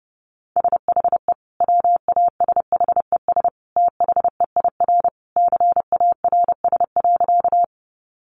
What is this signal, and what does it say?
Morse code, heard as a keyed tone, about 30 wpm, 700 Hz. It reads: SHE WASHES THEIR CARS.
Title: Easy Phrases 30wpm